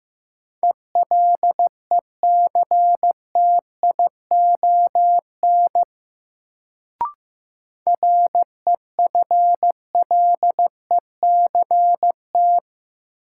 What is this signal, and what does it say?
Morse code, heard as a keyed tone, about 15 wpm, 700 Hz. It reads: ELECTION REFLECT
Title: Sentences from Top 1500 Words - New Word Drill Alt - 15wpm